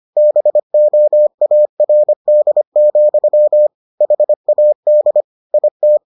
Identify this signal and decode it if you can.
Morse code, heard as a keyed tone, about 25 wpm, 600 Hz. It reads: BOARD, HAD IT